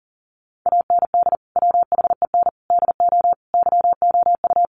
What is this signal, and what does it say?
Morse code, heard as a keyed tone, about 40 wpm, 700 Hz. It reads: AND WHEN DO YOU
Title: Sentences from Top 1100 Words 40wpm